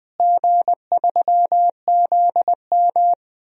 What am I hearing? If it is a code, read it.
Morse code, heard as a keyed tone, about 21 wpm, 700 Hz. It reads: G3ZM
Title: DXCC-100 Call Signs - 4 Characters - Repeated 3x - 20wpm